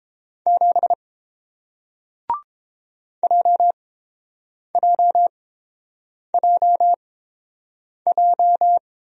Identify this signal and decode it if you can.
Morse code, heard as a keyed tone, about 30 wpm, spaced as for 7 wpm, 700 Hz. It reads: 7 JJJJ